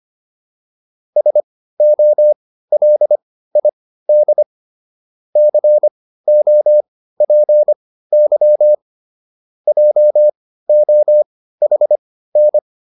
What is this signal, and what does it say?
Morse code, heard as a keyed tone, about 25 wpm, spaced as for 15 wpm, 600 Hz. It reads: SOLID COPY JOHN